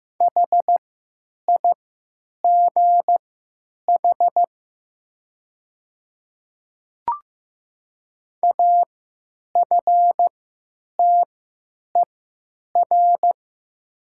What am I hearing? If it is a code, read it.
Morse code, heard as a keyed tone, about 15 wpm, spaced as for 8 wpm, 700 Hz. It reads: HIGH AFTER
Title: 3x Character Spacing - Top 200 Words - 15wpm